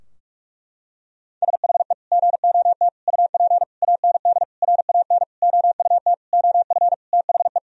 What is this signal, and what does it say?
Morse code, heard as a keyed tone, about 46 wpm, 700 Hz. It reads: SHE GOT UP AND RAN OUT OF THE
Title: Sentences from Top 1000 Words - Review Part 3 - 45wpm